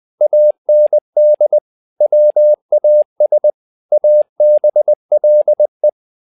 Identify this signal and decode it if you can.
Morse code, heard as a keyed tone, about 20 wpm, 600 Hz. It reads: AND WAS ABLE
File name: tower trouble 20wpm